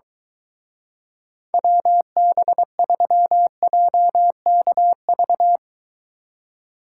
Morse code, heard as a keyed tone, about 23 wpm, 700 Hz. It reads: WB3JKV